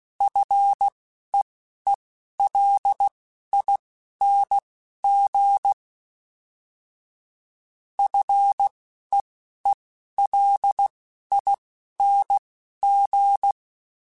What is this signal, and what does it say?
Morse code, heard as a keyed tone, about 16 wpm, spaced as for 11 wpm, 800 Hz. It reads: FEELING FEELING
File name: F and L / F and L mp3 / F and L Prefixes 3x2 Twice @10